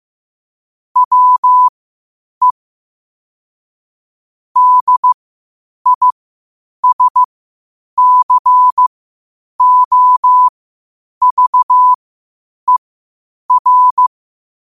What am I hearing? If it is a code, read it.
Morse code, heard as a keyed tone, about 15 wpm, spaced as for 8 wpm, 1000 Hz. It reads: WE DISCOVER
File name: qotdfiles / 8wpm